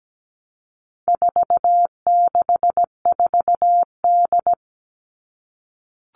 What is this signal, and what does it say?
Morse code, heard as a keyed tone, about 17 wpm, 700 Hz. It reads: 464D
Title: All Cut Numbers - 4-Digit 17wpm